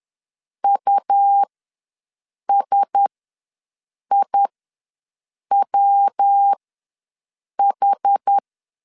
Morse code, heard as a keyed tone, about 10 wpm, spaced as for 6 wpm, 800 Hz. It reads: USIWH
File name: LEZIONI / 5 LBVC / 5 LB ws 10wpm 3616